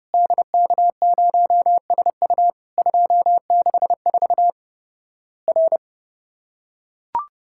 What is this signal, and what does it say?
Morse code, heard as a keyed tone, about 32 wpm, 700 Hz. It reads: DK0SU 264 R
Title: Worked All Europe DX Contest - QTC Exchange - 30wpm